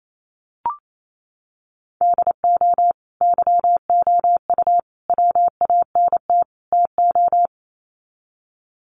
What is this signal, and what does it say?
Morse code, heard as a keyed tone, about 28 wpm, 700 Hz. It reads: DO YOU WANT TO
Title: Sets of 4 Words 28wpm